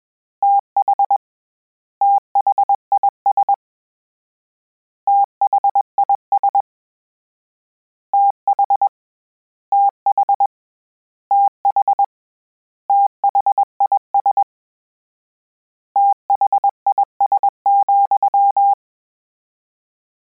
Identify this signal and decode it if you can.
Morse code, heard as a keyed tone, about 21 wpm, 800 Hz. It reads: TH THIS THIS TH TH TH THIS THIS,
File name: Combinations / Combinations-mp3s / Combinations@20